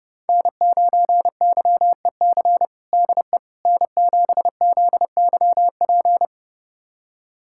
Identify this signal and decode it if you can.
Morse code, heard as a keyed tone, about 30 wpm, 700 Hz. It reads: N9YEC DE N7ZYP
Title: USA Callsign DE Callsign - 5 Characters - 30wpm